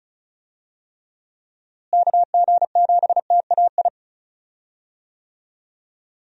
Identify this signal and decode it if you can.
Morse code, heard as a keyed tone, about 34 wpm, 700 Hz. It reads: KG7TAI
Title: USA Call Signs - 6 Characters - Repeated 3x - 35wpm